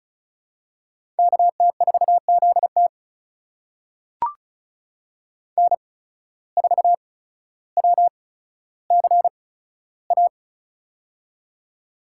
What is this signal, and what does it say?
Morse code, heard as a keyed tone, about 36 wpm, spaced as for 15 wpm, 700 Hz. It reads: KT4ZT N4WCA
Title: USA Call Signs - 5 Characters - 8x Character Spacing - 35wpm